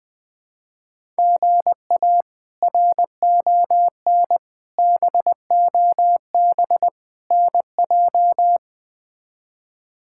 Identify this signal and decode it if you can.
Morse code, heard as a keyed tone, about 20 wpm, 700 Hz. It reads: GA RON BOB NJ
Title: K1USN SST QSO Practice - 20wpm